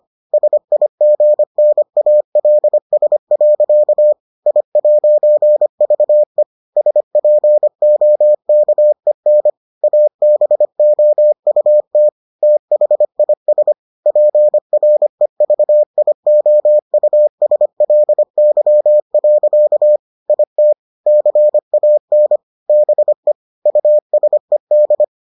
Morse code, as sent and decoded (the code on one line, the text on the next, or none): ... .. --. -. .- .-.. ... .-.-.- / .. .----. ...- . / ... .--. --- -.- . -. / .- -... --- ..- - / - .... .. ... / .--. .-. ...- .. --- ..- ... .-.. -.-- .-.-.- / .. - / -.-. .- -. / -... / ..- ... -..
SIGNALS. I'VE SPOKEN ABOUT THIS PREVIOUSLY. IT CAN BE USED